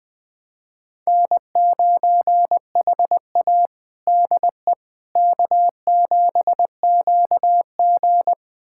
Morse code, heard as a keyed tone, about 20 wpm, 700 Hz. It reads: N9HA DE K7QG
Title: USA Callsign DE Callsign - 4 Characters - 20wpm